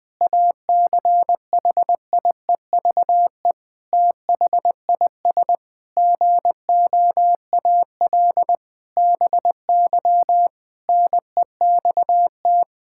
Morse code, heard as a keyed tone, about 20 wpm, 700 Hz. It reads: ACHIEVE THIS GOAL BY NEXT
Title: Sentences from Top 1000 Words - Review Part 5 - 20wpm